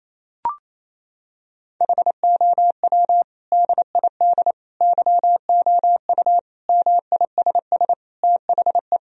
Morse code, heard as a keyed tone, about 28 wpm, 700 Hz. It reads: HOW DID YOU MISS THE
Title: Sentences from Top 1000 Words - Review Part 3 - 28wpm